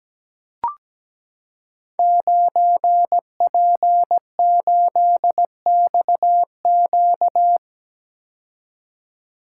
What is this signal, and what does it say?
Morse code, heard as a keyed tone, about 17 wpm, 700 Hz. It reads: 9P8XQ
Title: Code Groups - 5 Letter:Numbers - 17wpm